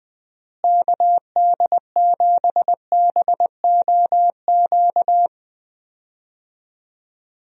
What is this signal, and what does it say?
Morse code, heard as a keyed tone, about 20 wpm, 700 Hz. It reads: KD7BOQ
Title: USA Call Signs - 6 Characters - Repeated 3x - 20wpm